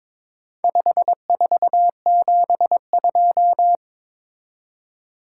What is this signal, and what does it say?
Morse code, heard as a keyed tone, about 22 wpm, 700 Hz. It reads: 5472
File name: practice / 22wpm / Four Letter-Number Combinations - 22wpm